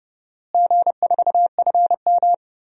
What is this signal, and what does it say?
Morse code, heard as a keyed tone, about 31 wpm, 700 Hz. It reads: G4FM
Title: DXCC-100 Call Signs - 4 Characters - Repeated 2x - 30wpm